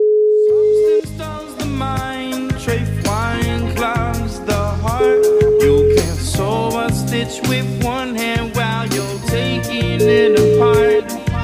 Ringback tone (Europe, 425 Hz), with noise.